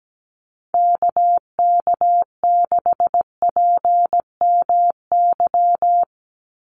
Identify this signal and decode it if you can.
Morse code, heard as a keyed tone, about 17 wpm, 700 Hz. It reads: KK6PMY
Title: USA Call Signs 17wpm